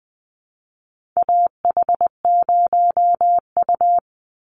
Morse code, heard as a keyed tone, about 19 wpm, 700 Hz. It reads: AH0U
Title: Call Signs - 20wpm